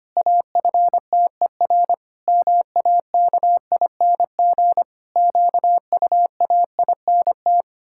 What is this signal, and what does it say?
Morse code, heard as a keyed tone, about 25 wpm, 700 Hz. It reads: AFTER MAKING QUAINT
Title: Pangram Challenge v2 - 25wpm